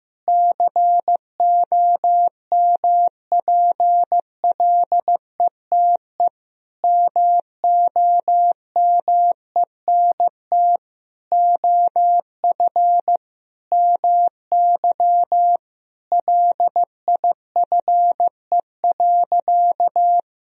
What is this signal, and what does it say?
Morse code, heard as a keyed tone, about 15 wpm, 700 Hz. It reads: COMPLETE MOMENT OF MY LIFE.
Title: Sentences from Top 1000 Words - Review Part 2 - 15wpm